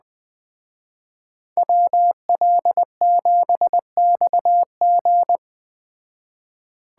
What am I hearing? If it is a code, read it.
Morse code, heard as a keyed tone, about 20 wpm, 700 Hz. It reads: WL7XG
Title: Speed Racing - USA Call Signs 17wpm